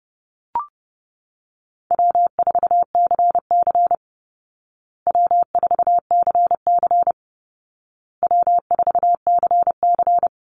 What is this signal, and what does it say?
Morse code, heard as a keyed tone, about 30 wpm, 700 Hz. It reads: W4CC W4CC W4CC
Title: Easy Call Signs Repeated 3x - 30wpm